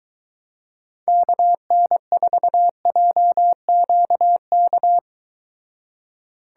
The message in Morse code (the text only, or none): KN4JQK